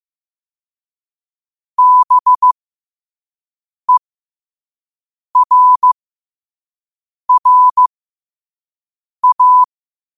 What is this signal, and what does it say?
Morse code, heard as a keyed tone, about 15 wpm, spaced as for 5 wpm, 1000 Hz. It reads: BERRA